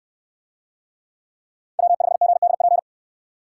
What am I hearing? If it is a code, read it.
Morse code, heard as a keyed tone, about 69 wpm, 700 Hz. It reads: W5GDP